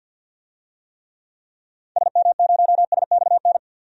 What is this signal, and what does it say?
Morse code, heard as a keyed tone, about 51 wpm, 700 Hz. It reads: SM0SXN
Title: DXCC-100 Call Signs - 6 Characters - 50wpm